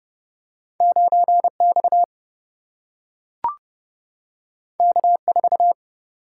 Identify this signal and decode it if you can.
Morse code, heard as a keyed tone, about 30 wpm, 700 Hz. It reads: 9X K4
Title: Single-Letter Single-Number 30wpm